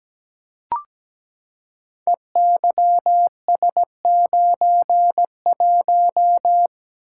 Morse code, heard as a keyed tone, about 17 wpm, 700 Hz. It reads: EYS91
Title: Code Groups - 5 Letter:Numbers - 17wpm